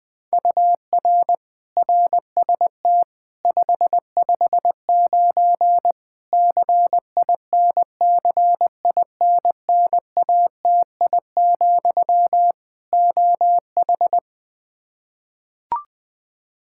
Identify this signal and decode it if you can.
Morse code, heard as a keyed tone, about 20 wpm, 700 Hz. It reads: UR RST 559 CINCINNATI, OH
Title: Practice QSO - Exchange 1 - Beginning - 20wpm